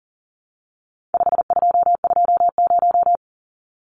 Morse code, heard as a keyed tone, about 40 wpm, 700 Hz. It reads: 5220
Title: Numbers - 4-Digit 40wpm